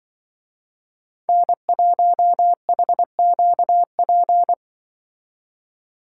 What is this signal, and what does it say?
Morse code, heard as a keyed tone, about 23 wpm, 700 Hz. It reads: N1HQP
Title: Speed Racing - USA 1x3 Call Signs 20wpm